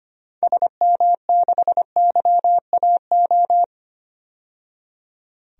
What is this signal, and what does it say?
Morse code, heard as a keyed tone, about 25 wpm, 700 Hz. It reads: SM6YAO
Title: DXCC-100 Call Signs - 6 Characters - 25wpm